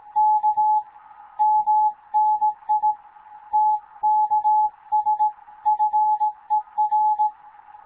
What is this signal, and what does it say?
Morse code, heard as a keyed tone, about 18 wpm, 860 Hz. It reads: K MNI TKS FER
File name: Morze-HA2MN / hangp / 2 11 GYAKORLAT